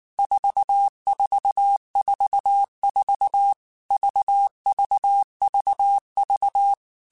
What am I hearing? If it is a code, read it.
Morse code, heard as a keyed tone, about 19 wpm, 800 Hz. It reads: 4444 VVVV
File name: Basic Letter Groups By7Groups / VB46 / VB46@18